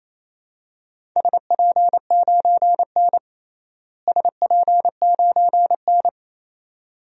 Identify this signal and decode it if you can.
Morse code, heard as a keyed tone, about 28 wpm, 700 Hz. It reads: SP9N SP9N
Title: DXCC-100 Call Signs - 4 Characters - Repeated 2x - 28wpm